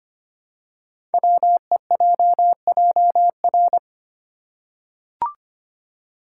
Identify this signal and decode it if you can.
Morse code, heard as a keyed tone, about 25 wpm, 700 Hz. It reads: WEJJR
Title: Code Groups - 5 Letters - 25wpm